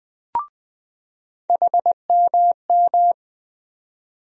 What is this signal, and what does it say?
Morse code, heard as a keyed tone, about 20 wpm, 700 Hz. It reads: HMM
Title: USA Call Sign Postfixes - 3 Letters 20wpm